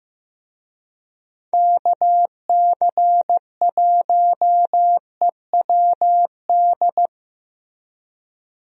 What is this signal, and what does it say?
Morse code, heard as a keyed tone, about 15 wpm, 700 Hz. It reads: KC1EWD
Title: USA Call Signs - 6 Characters - Repeated 2x - 15wpm